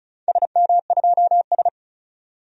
Morse code, heard as a keyed tone, about 36 wpm, 700 Hz. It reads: SM2S
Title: DXCC-100 Call Signs - 4 Characters - 35wpm